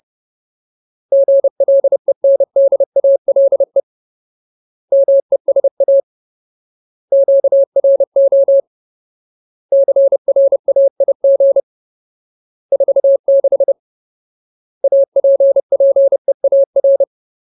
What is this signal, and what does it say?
Morse code, heard as a keyed tone, about 30 wpm, 550 Hz. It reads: GLENDALE MESA QRO CRAIG 46 APPEAR
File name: Ultimate Word List / 30WPM Ultimate